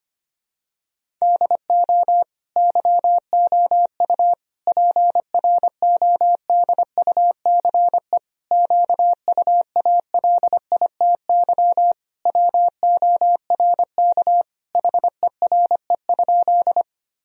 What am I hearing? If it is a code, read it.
Morse code, heard as a keyed tone, about 25 wpm, 700 Hz. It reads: DO YOU PRODUCE QUALITY WORK HERE?
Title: Sentences from Top 1000 Words - Review Part 4 - 25wpm